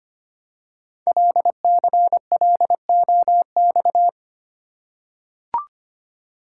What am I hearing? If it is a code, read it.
Morse code, heard as a keyed tone, about 25 wpm, 700 Hz. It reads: LCLOX
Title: Code Groups - 5 Letter:Numbers - 25wpm